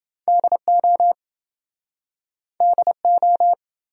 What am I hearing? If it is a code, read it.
Morse code, heard as a keyed tone, about 30 wpm, 700 Hz. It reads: DO DO